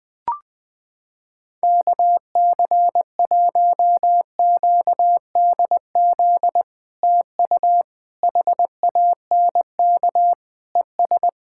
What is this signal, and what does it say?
Morse code, heard as a keyed tone, about 21 wpm, 700 Hz. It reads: KC1QDZ TU HANK ES
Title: POTA QSO Practice - 20wpm